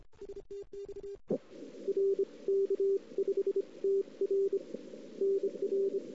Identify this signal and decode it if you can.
Morse code, heard as a keyed tone, about 28 wpm, 390 Hz. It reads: STX E R K5TR NR
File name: CW / 2008-ss-cw-k5tr-op-n5rz -02:00